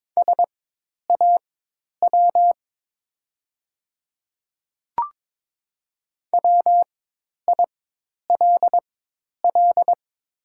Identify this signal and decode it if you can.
Morse code, heard as a keyed tone, about 22 wpm, spaced as for 10 wpm, 700 Hz. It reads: SAW WILL
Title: 4x Character Spacing - Top 500 Words - 22wpm